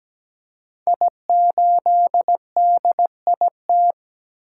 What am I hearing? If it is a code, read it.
Morse code, heard as a keyed tone, about 17 wpm, 700 Hz. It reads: I8DIT
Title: Code Groups - 5 Letter:Numbers - 17wpm